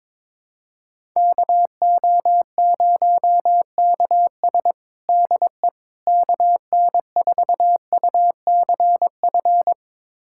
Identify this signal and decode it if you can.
Morse code, heard as a keyed tone, about 22 wpm, 700 Hz. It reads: KO0KS DE KN4UCF
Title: USA Callsign DE Callsign 22wpm